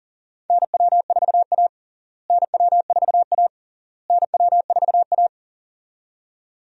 Morse code, heard as a keyed tone, about 40 wpm, 700 Hz. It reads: NW4A NW4A NW4A
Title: USA Call Signs - 4 Characters - Repeated 3x - 40wpm